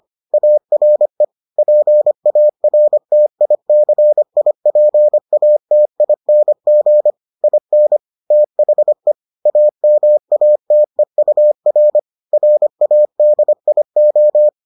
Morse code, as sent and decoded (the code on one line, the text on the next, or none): .- .-. / .--. .- .-. - .. -.-. .. .--. .- - .. -. --. / .. -. / - .... / .- -- .- - . ..- .-. / .-. .- -.. .. ---
ARE PARTICIPATING IN THE AMATEUR RADIO